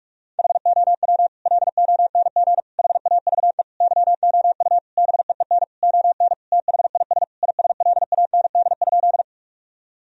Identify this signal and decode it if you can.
Morse code, heard as a keyed tone, about 46 wpm, 700 Hz. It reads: HOW LONG HAVE YOU BEEN ON THIS ISLAND?
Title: Sentences from Top 1000 Words - Review Part 3 - 45wpm